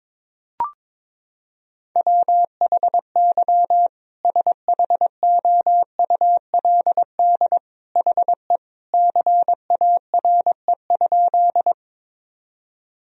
Morse code, heard as a keyed tone, about 22 wpm, 700 Hz. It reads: WHY SHOULD HE CARE?
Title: Sentences from Top 1000 Words - Review Part 1 - 22wpm